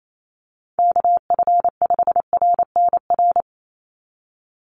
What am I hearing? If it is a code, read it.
Morse code, heard as a keyed tone, about 28 wpm, 700 Hz. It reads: KF5RNR
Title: USA Call Signs 28wpm